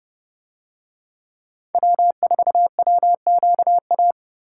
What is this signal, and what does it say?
Morse code, heard as a keyed tone, about 30 wpm, 700 Hz. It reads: W4WQA